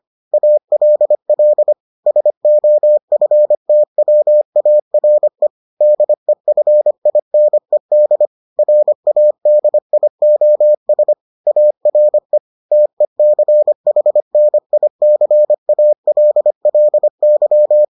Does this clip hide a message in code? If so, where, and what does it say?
Yes: Morse code from 0.3 to 17.8 s, heard as a keyed tone, about 25 wpm, 600 Hz. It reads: ALL SOFTWARE DEFINED RADIOS ARE TECHNICALLY